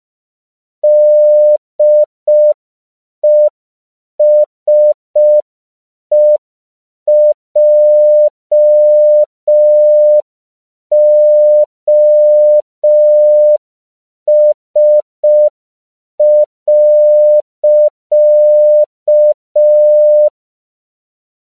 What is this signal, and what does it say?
Morse code, heard as a keyed tone, about 5 wpm, 600 Hz. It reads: DESEJOS.
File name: CW / 5PPM